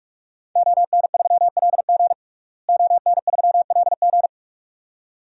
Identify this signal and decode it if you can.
Morse code, heard as a keyed tone, about 46 wpm, 700 Hz. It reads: ON3LG ON3LG
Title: DXCC-100 Call Signs - 5 Characters - Repeated 2x - 45wpm